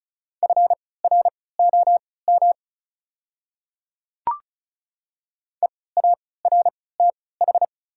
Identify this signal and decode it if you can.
Morse code, heard as a keyed tone, about 36 wpm, spaced as for 20 wpm, 700 Hz. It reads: FROM EARTH
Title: 3x Character Spacing - Top 300 Words - 35wpm